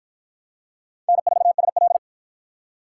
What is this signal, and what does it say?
Morse code, heard as a keyed tone, about 46 wpm, 700 Hz. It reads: N4SL